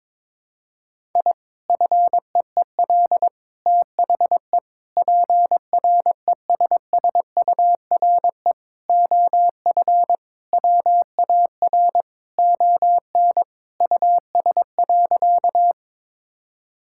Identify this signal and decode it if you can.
Morse code, heard as a keyed tone, about 22 wpm, 700 Hz. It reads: I FEEL THE PRESSURE OF WAR ON US.
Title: Sentences from Top 1000 Words - Review Part 6 - 22wpm